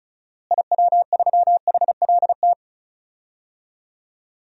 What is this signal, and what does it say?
Morse code, heard as a keyed tone, about 36 wpm, 700 Hz. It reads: IW3HLT
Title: DXCC-100 Call Signs - 6 Characters - Repeated 3x - 35wpm